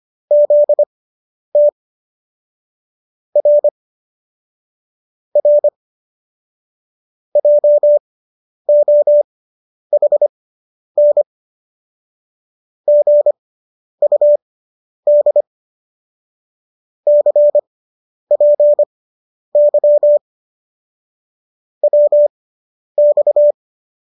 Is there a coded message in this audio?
Yes: Morse code from 0.3 to 23.5 s, heard as a keyed tone, about 25 wpm, spaced as for 10 wpm, 590 Hz. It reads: ZT R R JOHN GUD CPY WX